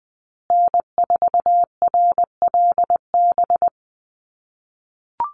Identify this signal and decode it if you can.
Morse code, heard as a keyed tone, about 20 wpm, 700 Hz. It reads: N4RLB
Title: USA Call Signs Repeated 2x 20wpm